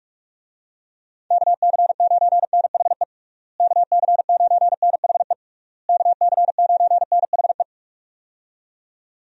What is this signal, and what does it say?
Morse code, heard as a keyed tone, about 46 wpm, 700 Hz. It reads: KC9NHE KC9NHE KC9NHE
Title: USA Call Signs - 6 Characters - Repeated 3x - 45wpm